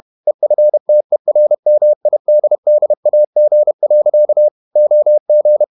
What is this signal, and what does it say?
Morse code, heard as a keyed tone, about 31 wpm, 600 Hz. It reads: EFTERMIDDAG. OG
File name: Ferie QRQ 31wpm